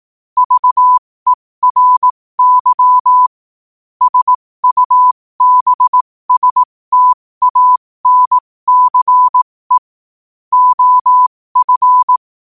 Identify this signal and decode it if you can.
Morse code, heard as a keyed tone, about 18 wpm, spaced as for 15 wpm, 1000 Hz. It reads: VERY SUBSTANCE OF
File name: qotdfiles / 15wpm